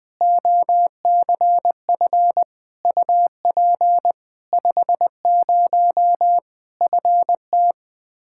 Morse code, heard as a keyed tone, about 20 wpm, 700 Hz. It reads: OCF UP 50 FT
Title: Practice QSO - Exchange 2 - Info - 20wpm